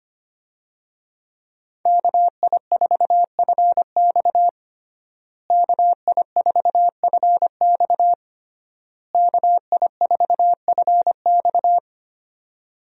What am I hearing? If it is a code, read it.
Morse code, heard as a keyed tone, about 25 wpm, 700 Hz. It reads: KI4FX KI4FX KI4FX
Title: USA Call Signs - 5 Characters - Repeated 3x - 25wpm